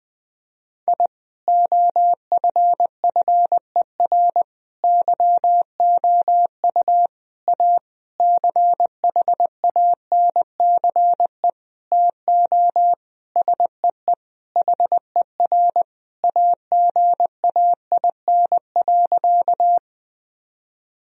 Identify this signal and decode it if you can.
Morse code, heard as a keyed tone, about 21 wpm, 700 Hz. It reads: I OFFER YOU A CHANCE TO SEE HER AGAIN.
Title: Sentences from Top 1000 Words - Review Part 4 - 20wpm